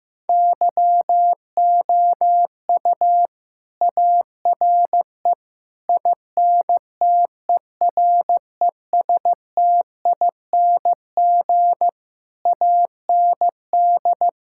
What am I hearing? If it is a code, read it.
Morse code, heard as a keyed tone, about 15 wpm, 700 Hz. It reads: YOU ARE INTERESTING AND
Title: Sentences from Top 1000 Words - Review Part 6 - 15wpm